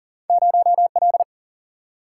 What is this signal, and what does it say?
Morse code, heard as a keyed tone, about 40 wpm, 700 Hz. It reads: 0L
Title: Single-Letter Single-Number 40wpm